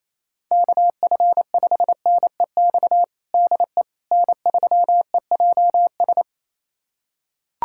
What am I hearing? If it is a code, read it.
Morse code, heard as a keyed tone, about 28 wpm, 700 Hz. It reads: KF5NEX DE N3EJS
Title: USA Callsign DE Callsign 28wpm